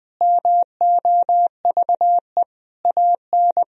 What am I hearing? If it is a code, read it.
Morse code, heard as a keyed tone, about 20 wpm, 700 Hz. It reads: MOVE AN
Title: Sentences from Top 1000 Words - Review Part 3 - 20wpm